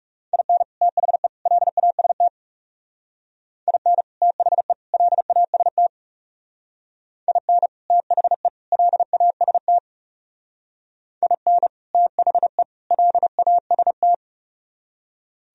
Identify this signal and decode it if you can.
Morse code, heard as a keyed tone, about 35 wpm, 700 Hz. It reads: IN THE LAST IN THE LAST IN THE LAST IN THE LAST
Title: Speed-Racing - Sets of 3 Words 30wpm